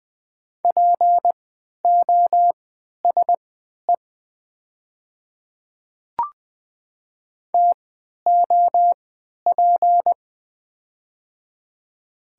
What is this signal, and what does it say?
Morse code, heard as a keyed tone, about 20 wpm, spaced as for 11 wpm, 700 Hz. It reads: POSE TOP
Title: 3x Character Spacing - Top 500 Words - 20wpm